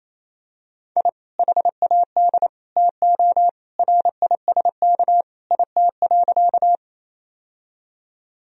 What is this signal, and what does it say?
Morse code, heard as a keyed tone, about 28 wpm, 700 Hz. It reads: I HAD TO RISK IT.
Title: Sentences from Top 1000 Words - Review Part 4 - 28wpm